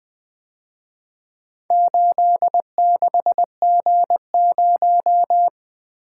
Morse code, heard as a keyed tone, about 23 wpm, 700 Hz. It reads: 86G0
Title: Speed-Racing - Four Letter-Number Combinations - 20wpm